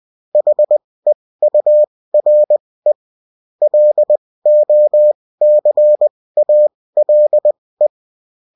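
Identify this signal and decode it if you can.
Morse code, heard as a keyed tone, about 20 wpm, spaced as for 16 wpm, 600 Hz. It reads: HEURE LOCALE
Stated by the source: Morse code bulletin of a French amateur CW union, sent as text and in CW.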